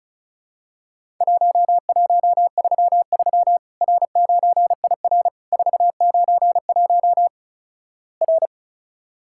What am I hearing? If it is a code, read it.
Morse code, heard as a keyed tone, about 36 wpm, 700 Hz. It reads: 1133 R9IR 491 R
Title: Worked All Europe DX Contest - QTC Exchange - 35wpm